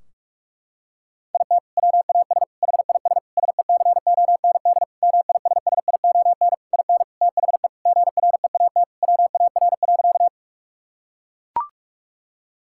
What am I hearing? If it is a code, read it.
Morse code, heard as a keyed tone, about 41 wpm, 700 Hz. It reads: IT WAS HIS SECOND MISSION IN THE GREAT WAR.
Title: Sentences from Top 1600 Words 45wpm